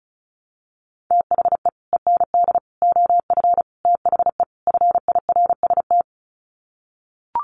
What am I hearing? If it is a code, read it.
Morse code, heard as a keyed tone, about 34 wpm, 700 Hz. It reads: THE END OF THE FIRST
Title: Sets of 5 Words Encore 35wpm